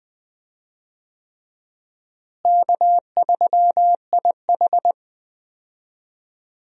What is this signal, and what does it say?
Morse code, heard as a keyed tone, about 21 wpm, 700 Hz. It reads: K3IH
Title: USA Call Signs - 4 Characters - Repeated 3x - 20wpm